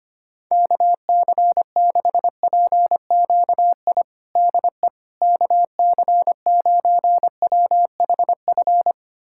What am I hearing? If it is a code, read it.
Morse code, heard as a keyed tone, about 25 wpm, 700 Hz. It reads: KC6PQI DE KC9WHF
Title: USA Callsign DE Callsign - 6 Characters - 25wpm